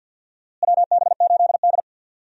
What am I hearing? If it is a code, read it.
Morse code, heard as a keyed tone, about 50 wpm, 700 Hz. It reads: WB8D
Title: USA Call Signs - 4 Characters - Repeated 3x - 50wpm